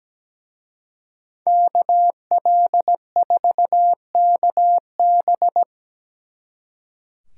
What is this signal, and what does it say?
Morse code, heard as a keyed tone, about 20 wpm, 700 Hz. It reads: KL4KB
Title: Speed Racing - USA 2x2 Call Signs 17wpm